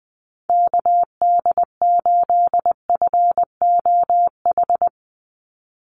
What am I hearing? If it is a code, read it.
Morse code, heard as a keyed tone, about 20 wpm, 700 Hz. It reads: KD8FOH
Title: USA Call Signs 20wpm